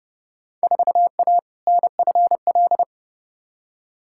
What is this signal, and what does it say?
Morse code, heard as a keyed tone, about 30 wpm, 700 Hz. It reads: 4A NFL
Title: Field Day Exchanges - 30wpm